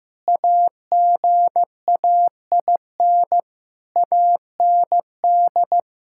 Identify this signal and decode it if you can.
Morse code, heard as a keyed tone, about 15 wpm, 700 Hz. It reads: AGAIN AND
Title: Pangram Challenge v2 - 15wpm